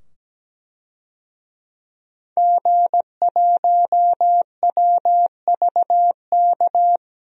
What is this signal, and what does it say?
Morse code, heard as a keyed tone, about 17 wpm, 700 Hz. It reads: G1WVK
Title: DXCC-100 Call Signs - 5 Characters - 17wpm